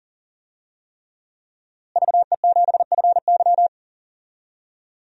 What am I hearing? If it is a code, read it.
Morse code, heard as a keyed tone, about 40 wpm, 700 Hz. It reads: VE7FY
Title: DXCC-100 Call Signs - 5 Characters - Repeated 3x - 40wpm